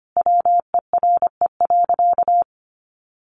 Morse code, heard as a keyed tone, about 25 wpm, 700 Hz. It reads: WERE.
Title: Sentences from Top 1000 Words 25wpm